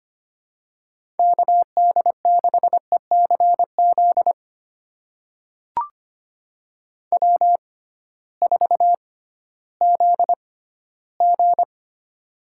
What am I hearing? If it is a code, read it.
Morse code, heard as a keyed tone, about 25 wpm, spaced as for 15 wpm, 700 Hz. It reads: KD6ECZ W4ZG
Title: USA Call Signs - 6x Character Spacing - 25wpm